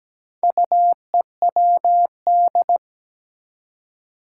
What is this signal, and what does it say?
Morse code, heard as a keyed tone, about 17 wpm, 700 Hz. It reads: UEWD
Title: Four Challenging Letter-Number Combinations 17wpm